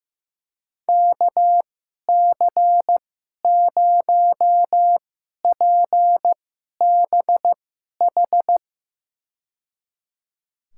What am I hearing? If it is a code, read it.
Morse code, heard as a keyed tone, about 16 wpm, spaced as for 11 wpm, 700 Hz. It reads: KC0PBH
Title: USA Call Signs - 2x Character Spacing - 15wpm